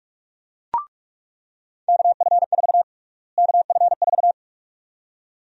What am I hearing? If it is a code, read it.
Morse code, heard as a keyed tone, about 46 wpm, 700 Hz. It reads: KF4 KF4
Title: USA Call Sign Prefixes - Repeated 2x - 45wpm